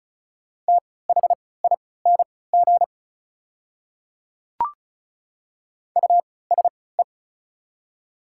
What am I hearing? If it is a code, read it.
Morse code, heard as a keyed tone, about 36 wpm, spaced as for 20 wpm, 690 Hz. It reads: THING USE
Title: 3x Character Spacing - Top 100 Words - 35wpm